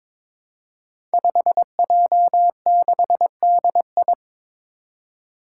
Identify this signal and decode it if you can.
Morse code, heard as a keyed tone, about 22 wpm, 700 Hz. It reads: 5J6DI
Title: Five Challenging Letter-Number Combinations 22wpm